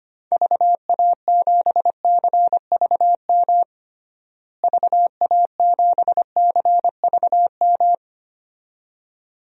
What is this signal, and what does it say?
Morse code, heard as a keyed tone, about 25 wpm, 700 Hz. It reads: VA7CVM VA7CVM
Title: DXCC-100 Call Signs - 6 Characters - Repeated 2x - 25wpm